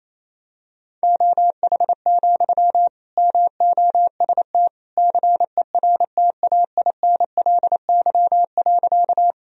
Morse code, heard as a keyed tone, about 28 wpm, 700 Hz. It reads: OH, MOST CERTAINLY.
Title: Sentences from Top 1000 Words - Review Part 1 - 28wpm